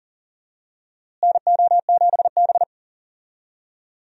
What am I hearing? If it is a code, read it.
Morse code, heard as a keyed tone, about 40 wpm, 700 Hz. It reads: NO7B
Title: USA Call Signs - 4 Characters - Repeated 2x - 40wpm